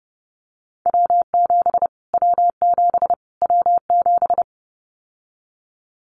Morse code, heard as a keyed tone, about 30 wpm, 700 Hz. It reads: W7 W7 W7
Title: Callsign Prefixes for Top 25 DXCC - Repeated 3x - 30wpm